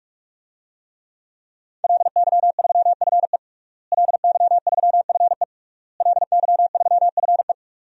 Morse code, heard as a keyed tone, about 47 wpm, 700 Hz. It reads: LY3FE LY3FE LY3FE
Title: DXCC-100 Call Signs - 5 Characters - Repeated 3x - 45wpm